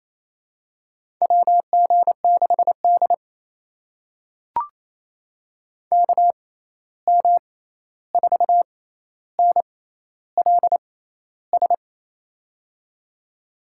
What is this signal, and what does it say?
Morse code, heard as a keyed tone, about 28 wpm, spaced as for 13 wpm, 700 Hz. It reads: WG6D KM4NLS
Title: USA Call Signs - 6x Character Spacing - 28wpm